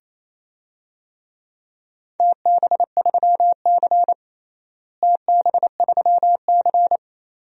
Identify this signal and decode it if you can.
Morse code, heard as a keyed tone, about 28 wpm, 700 Hz. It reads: TB3C TB3C
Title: DXCC-100 Call Signs - 4 Characters - Repeated 2x - 28wpm